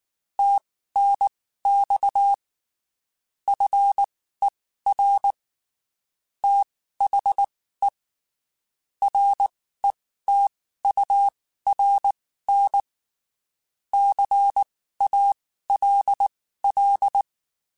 Morse code, heard as a keyed tone, about 19 wpm, spaced as for 14 wpm, 800 Hz. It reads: TNX FER THE RETURN CALL
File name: (spaced) / QSOs Spaced-mp3s / kc7haa @18 sp